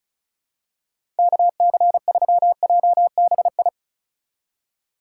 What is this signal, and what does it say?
Morse code, heard as a keyed tone, about 36 wpm, 700 Hz. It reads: KC3JBI
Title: USA Call Signs - 4x Character Spacing - 35wpm